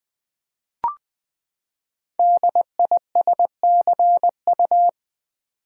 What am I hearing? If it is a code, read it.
Morse code, heard as a keyed tone, about 20 wpm, 700 Hz. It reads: DISCU
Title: Build A Word - 5 Letters At A Time - 10-Letter Words - 20wpm